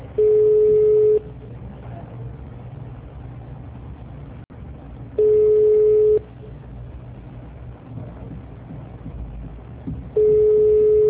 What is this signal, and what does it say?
Ringback tone (Europe, 425 Hz).